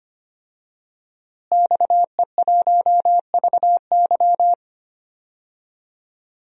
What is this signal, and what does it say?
Morse code, heard as a keyed tone, about 25 wpm, 700 Hz. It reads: XE1VY
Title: DXCC-100 Call Signs - 5 Characters - Repeated 2x - 25wpm